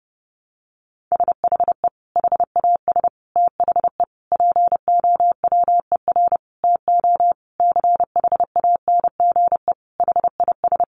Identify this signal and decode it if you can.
Morse code, heard as a keyed tone, about 30 wpm, 700 Hz. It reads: SHE HAS THE POWER TO CHANGE HIS
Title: Sentences from Top 1200 Words 30wpm